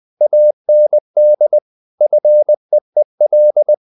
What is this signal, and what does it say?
Morse code, heard as a keyed tone, about 20 wpm, 600 Hz. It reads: AND FEEL
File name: the old man and the sea part 2 20wpm